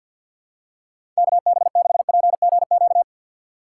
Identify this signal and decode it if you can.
Morse code, heard as a keyed tone, about 53 wpm, 700 Hz. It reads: KB6PGQ
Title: USA Call Signs - 4x Character Spacing - 50wpm